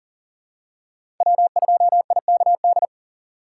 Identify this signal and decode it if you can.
Morse code, heard as a keyed tone, about 40 wpm, 700 Hz. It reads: W2IKD
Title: USA Call Signs - 4x Character Spacing - 40wpm